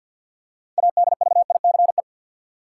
Morse code, heard as a keyed tone, about 53 wpm, 700 Hz. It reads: ADVICE
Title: Six Letter Words - 50wpm